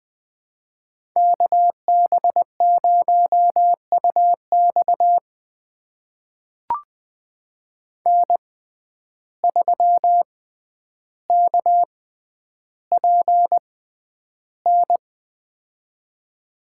Morse code, heard as a keyed tone, about 20 wpm, spaced as for 10 wpm, 700 Hz. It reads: KB0UX N3KPN
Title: USA Call Signs - 5 Characters - 6x Character Spacing - 20wpm